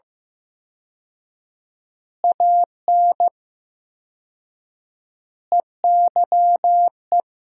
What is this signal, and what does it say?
Morse code, heard as a keyed tone, about 15 wpm, 700 Hz. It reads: AN EYE
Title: 4x Word-Spacing - Sentences from Top 300 Words - 15wpm